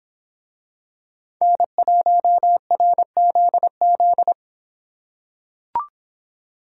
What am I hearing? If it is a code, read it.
Morse code, heard as a keyed tone, about 25 wpm, 700 Hz. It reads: N1RZZ